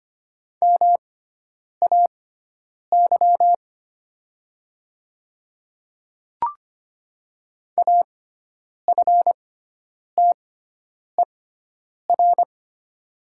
Morse code, heard as a keyed tone, about 25 wpm, spaced as for 9 wpm, 700 Hz. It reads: MAY AFTER
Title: 6x Character Spacing - Top 200 Words - 25wpm